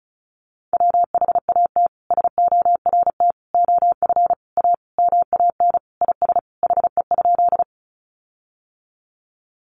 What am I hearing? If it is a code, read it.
Morse code, heard as a keyed tone, about 34 wpm, 700 Hz. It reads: WHAT SORT OF A MAN IS HE?
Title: Sentences from Top 1300 Words 35wpm